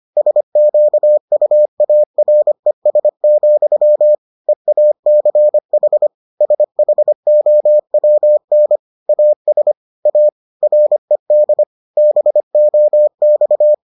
Morse code, heard as a keyed tone, about 25 wpm, 600 Hz. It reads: SQUARES, EACH SHOWN AS A RED BOX